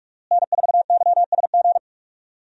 Morse code, heard as a keyed tone, about 44 wpm, 700 Hz. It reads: N4YSG
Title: USA Call Signs - 5 Characters - Repeated 3x - 45wpm